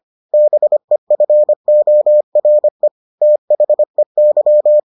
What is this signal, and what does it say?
Morse code, heard as a keyed tone, about 25 wpm, 600 Hz. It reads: BEFORE THEY